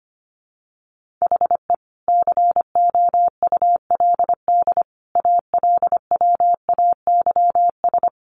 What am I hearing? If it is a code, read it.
Morse code, heard as a keyed tone, about 25 wpm, 700 Hz. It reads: HE COULD ALWAYS
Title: Sentences from Top 1400 Words 25wpm